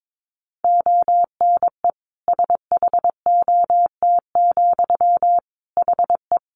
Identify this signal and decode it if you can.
Morse code, heard as a keyed tone, about 22 wpm, 700 Hz. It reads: ONE SHOT, HE
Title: Sentences from Top 1100 Words 22wpm